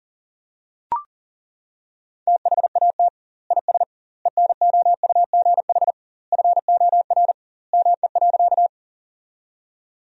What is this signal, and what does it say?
Morse code, heard as a keyed tone, about 40 wpm, 700 Hz. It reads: THAT IS ENOUGH FOR ME.
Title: 2x Word-Spacing - Sentences from Top 400 Words - 40wpm